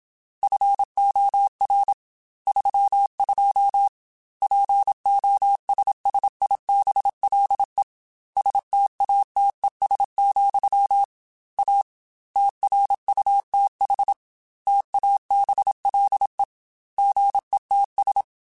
Morse code, heard as a keyed tone, about 27 wpm, 800 Hz. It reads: FOR 32 POSSIBLE STATES, A TRUTH TABLE GETS
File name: Technical Subjects / Technical Subjects-mp3s / AND Gate @25